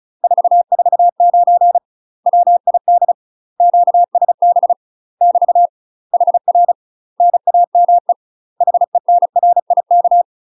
Morse code, heard as a keyed tone, about 35 wpm, 700 Hz. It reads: 449 WID QSB <BT> HR NAME HENRIK
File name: UK MASTER QSOs TRAINING / 35WPM / UK MASTER QSOs 0032